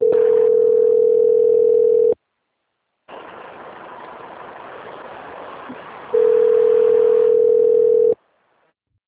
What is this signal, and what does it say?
Ringback tone (North America), 440 + 480 Hz.
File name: out- - - - - .1864